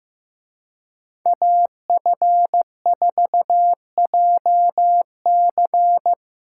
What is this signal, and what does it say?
Morse code, heard as a keyed tone, about 18 wpm, 700 Hz. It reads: AF4JC
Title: Speed Racing - USA 2x2 Call Signs 15wpm